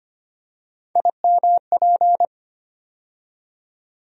Morse code, heard as a keyed tone, about 25 wpm, 700 Hz. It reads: IMP
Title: USA Call Sign Postfixes - 3 Letters 25wpm